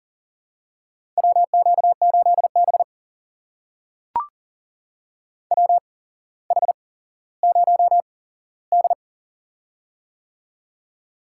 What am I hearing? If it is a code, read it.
Morse code, heard as a keyed tone, about 38 wpm, spaced as for 17 wpm, 700 Hz. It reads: WQ8B WH0D
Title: USA Call Signs - 4 Characters - 8x Character Spacing - 40wpm